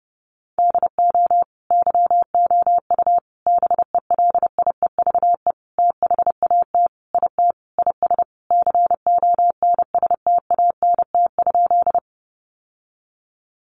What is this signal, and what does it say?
Morse code, heard as a keyed tone, about 28 wpm, 700 Hz. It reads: DO YOU BELIEVE THAT IT IS CONSTANT?
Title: Sentences from Top 1500 Words 30wpm